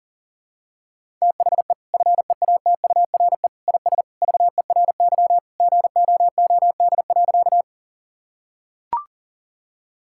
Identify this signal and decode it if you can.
Morse code, heard as a keyed tone, about 40 wpm, 700 Hz. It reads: THE FEATURE IS VERY GOOD.
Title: Sentences from Top 1000 Words - Review Part 5 - 40wpm